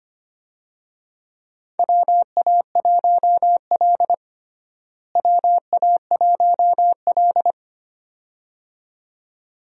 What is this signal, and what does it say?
Morse code, heard as a keyed tone, about 25 wpm, 700 Hz. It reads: WA1L WA1L
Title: USA Call Signs - 4 Characters - Repeated 2x - 25wpm